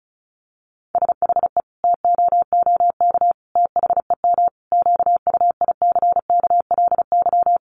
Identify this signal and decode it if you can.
Morse code, heard as a keyed tone, about 34 wpm, 700 Hz. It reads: SHE TOOK THEM QUICKLY
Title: Sentences from Top 1300 Words 35wpm